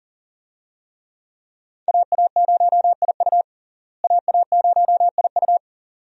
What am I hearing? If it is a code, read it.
Morse code, heard as a keyed tone, about 40 wpm, 700 Hz. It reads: AA0IU AA0IU
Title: USA Call Signs - 5 Characters - Repeated 2x - 40wpm